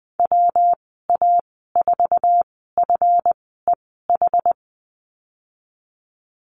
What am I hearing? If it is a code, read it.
Morse code, heard as a keyed tone, about 19 wpm, spaced as for 14 wpm, 700 Hz. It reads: WA4FEH
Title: 154 Lesson 25g - Letter p - Call Signs 20wpm